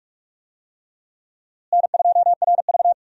Morse code, heard as a keyed tone, about 46 wpm, 700 Hz. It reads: N2RV
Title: USA Call Signs - 4 Characters - Repeated 3x - 45wpm